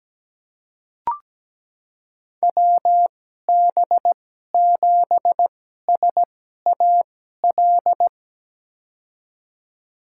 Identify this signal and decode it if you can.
Morse code, heard as a keyed tone, about 17 wpm, spaced as for 12 wpm, 700 Hz. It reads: WB7SAL
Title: USA Call Signs - 6 Characters - 2x Character Spacing - 17wpm